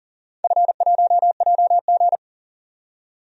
Morse code, heard as a keyed tone, about 40 wpm, 700 Hz. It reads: F1JG
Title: DXCC-100 Call Signs - 4 Characters - 40wpm